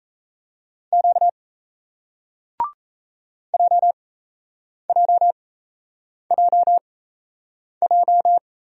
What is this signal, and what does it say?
Morse code, heard as a keyed tone, about 37 wpm, spaced as for 8 wpm, 700 Hz. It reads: Q JJJJ